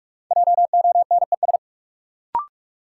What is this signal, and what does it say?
Morse code, heard as a keyed tone, about 42 wpm, 700 Hz. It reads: JONES
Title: Sets of 5 Words - New Word Drill - 45wpm